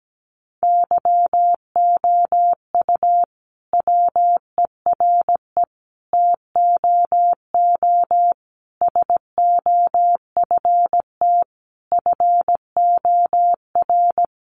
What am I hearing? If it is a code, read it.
Morse code, heard as a keyed tone, about 17 wpm, 700 Hz. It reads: YOU WERE TOO SOFT FOR
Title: Sentences from Top 1500 Words 17wpm